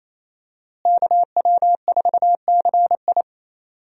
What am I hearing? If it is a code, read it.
Morse code, heard as a keyed tone, about 27 wpm, 700 Hz. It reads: KW4CI
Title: USA Call Signs - 5 Characters - Repeated 2x - 28wpm